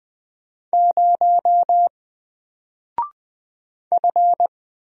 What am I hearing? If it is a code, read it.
Morse code, heard as a keyed tone, about 20 wpm, 700 Hz. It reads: 0 F
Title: Single Character (Letter:Number:Punctuation:Prosign) Alt 20wpm